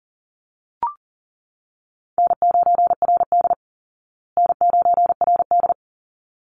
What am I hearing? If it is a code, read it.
Morse code, heard as a keyed tone, about 40 wpm, 700 Hz. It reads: N9RD N9RD
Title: Easy Call Signs Repeated 2x - 40wpm